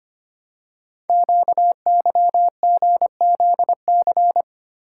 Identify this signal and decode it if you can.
Morse code, heard as a keyed tone, about 25 wpm, 700 Hz. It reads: QYGZC
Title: Code Groups - 5 Letters - 25wpm